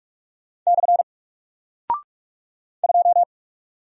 Morse code, heard as a keyed tone, about 44 wpm, 700 Hz. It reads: / 2
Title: Single Character (Letter:Number:Punctuation:Prosign) 45wpm